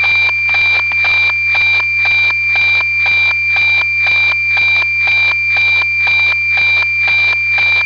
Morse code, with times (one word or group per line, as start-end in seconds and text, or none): none